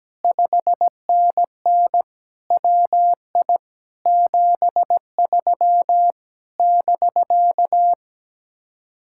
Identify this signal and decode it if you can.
Morse code, heard as a keyed tone, about 17 wpm, 700 Hz. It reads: 5NN WI 73 <BK>
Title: Speed-Racing Common QSO Phrases - 17wpm